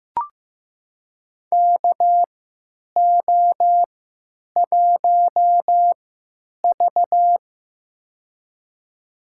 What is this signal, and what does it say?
Morse code, heard as a keyed tone, about 15 wpm, spaced as for 8 wpm, 700 Hz. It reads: KO1V
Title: USA Call Signs - 4 Characters - 3x Character Spacing - 15wpm